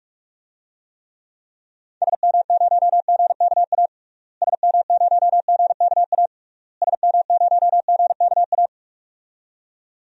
Morse code, heard as a keyed tone, about 41 wpm, 700 Hz. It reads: SM0GKA SM0GKA SM0GKA
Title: DXCC-100 Call Signs - 6 Characters - Repeated 3x - 45wpm